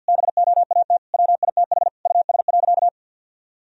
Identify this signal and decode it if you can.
Morse code, heard as a keyed tone, about 50 wpm, 700 Hz. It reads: BOAT WITH US.
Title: Sentences from Top 1000 Words - Review Part 3 - 50wpm